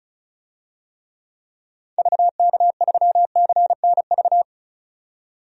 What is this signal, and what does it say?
Morse code, heard as a keyed tone, about 34 wpm, 700 Hz. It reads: VK3CNV
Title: DXCC-100 Call Signs - 6 Characters - 35wpm